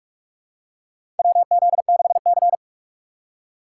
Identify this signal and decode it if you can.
Morse code, heard as a keyed tone, about 46 wpm, 700 Hz. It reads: WZ6C
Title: USA Call Signs - 4 Characters - 2x Character Spacing - 45wpm